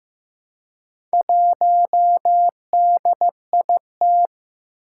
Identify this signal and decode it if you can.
Morse code, heard as a keyed tone, about 15 wpm, 700 Hz. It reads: 1DIT
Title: Four Challenging Letter-Number Combinations 15wpm